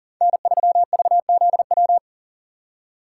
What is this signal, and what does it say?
Morse code, heard as a keyed tone, about 40 wpm, 700 Hz. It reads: N3VZW
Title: USA Call Signs - 5 Characters - 3x Character Spacing - 40wpm